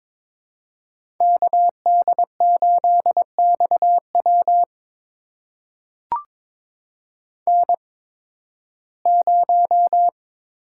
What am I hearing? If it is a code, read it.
Morse code, heard as a keyed tone, about 22 wpm, spaced as for 14 wpm, 700 Hz. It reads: KD8XW N0
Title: USA Call Signs - 5 Characters - 8x Character Spacing - 22wpm